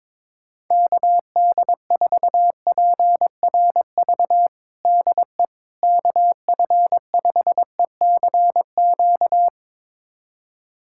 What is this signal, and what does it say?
Morse code, heard as a keyed tone, about 22 wpm, 700 Hz. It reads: KD4PRV DE KF5ECQ
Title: USA Callsign DE Callsign - 6 Characters - 22wpm